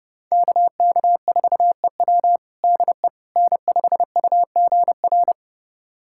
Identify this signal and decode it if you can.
Morse code, heard as a keyed tone, about 30 wpm, 700 Hz. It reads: KK4EW DE N5UGR
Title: USA Callsign DE Callsign - 5 Characters - 30wpm